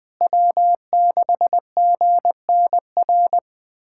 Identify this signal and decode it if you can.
Morse code, heard as a keyed tone, about 20 wpm, 700 Hz. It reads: W6GNR
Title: USA Callsign DE Callsign - 5 Characters - 20wpm